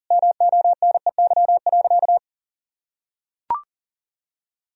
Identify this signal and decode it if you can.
Morse code, heard as a keyed tone, about 37 wpm, 700 Hz. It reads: MONEY.
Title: Sentences from Top 1000 Words - Review Part 3 - 40wpm